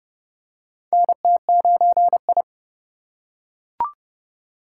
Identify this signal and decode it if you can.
Morse code, heard as a keyed tone, about 29 wpm, 700 Hz. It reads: NT9I
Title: USA Call Signs - 2x Character Spacing - 30wpm